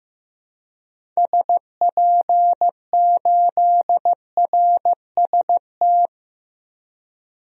Morse code, heard as a keyed tone, about 16 wpm, 700 Hz. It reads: SP8RST
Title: DXCC-100 Call Signs - 6 Characters - 15wpm